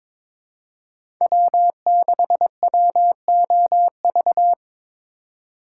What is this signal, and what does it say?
Morse code, heard as a keyed tone, about 22 wpm, 700 Hz. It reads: W6WOV
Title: USA Call Signs - 5 Characters - 4x Character Spacing - 22wpm